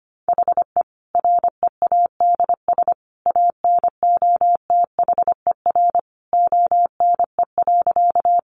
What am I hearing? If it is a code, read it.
Morse code, heard as a keyed tone, about 25 wpm, 700 Hz. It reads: HE READS ANOTHER ONE.
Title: Additional Easy Phrases 25wpm